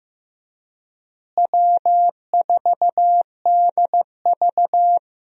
Morse code, heard as a keyed tone, about 15 wpm, 700 Hz. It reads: W4DV
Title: USA Call Signs - 4 Characters - Repeated 2x - 15wpm